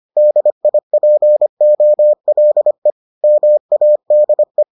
Morse code, heard as a keyed tone, about 25 wpm, 600 Hz. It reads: DIPOLE MADE